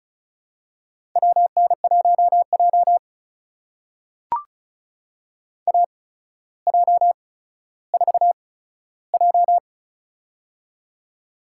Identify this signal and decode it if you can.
Morse code, heard as a keyed tone, about 36 wpm, spaced as for 15 wpm, 700 Hz. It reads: WN1J AJ4J
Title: USA Call Signs - 4 Characters - 8x Character Spacing - 35wpm